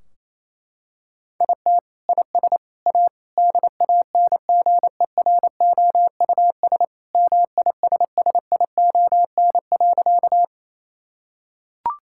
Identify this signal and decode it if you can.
Morse code, heard as a keyed tone, about 28 wpm, 700 Hz. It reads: IT IS A DANGEROUS MISSION.
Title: Sentences from Top 1600 Words 28wpm